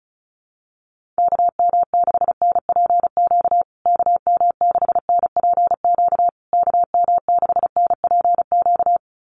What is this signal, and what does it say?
Morse code, heard as a keyed tone, about 34 wpm, 700 Hz. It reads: KM6NPQ KM6NPQ KM6NPQ
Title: USA Call Signs Repeated 3x 35wpm